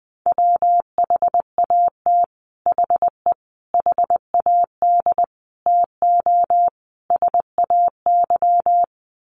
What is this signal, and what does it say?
Morse code, heard as a keyed tone, about 20 wpm, 700 Hz. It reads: WHAT HE HAD TO SAY
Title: Sets of 5 Words Encore2 20wpm